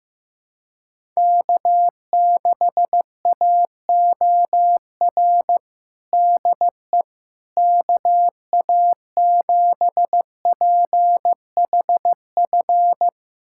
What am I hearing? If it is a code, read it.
Morse code, heard as a keyed tone, about 16 wpm, 700 Hz. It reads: K6AOR DE KA7PHF
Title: USA Callsign DE Callsign 15wpm